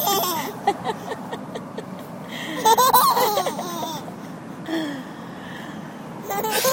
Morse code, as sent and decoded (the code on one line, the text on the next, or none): none